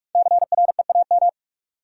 Morse code, heard as a keyed tone, about 47 wpm, 700 Hz. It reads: CREAM
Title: Sentences from Top 1500 Words - New Word Drill - 45wpm